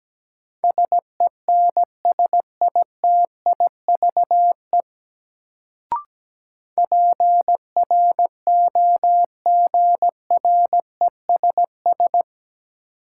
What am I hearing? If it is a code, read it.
Morse code, heard as a keyed tone, about 17 wpm, 700 Hz. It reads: SENSITIVE PROGRESS
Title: Sentences from Top 1400 Words - New Word Drill - 17wpm